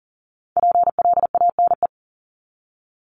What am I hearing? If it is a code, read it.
Morse code, heard as a keyed tone, about 40 wpm, 700 Hz. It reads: PLANE
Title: Top 500 Words 40wpm